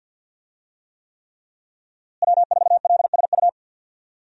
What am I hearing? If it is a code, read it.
Morse code, heard as a keyed tone, about 52 wpm, 700 Hz. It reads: W4LSU
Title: USA Call Signs - 5 Characters - Repeated 3x - 50wpm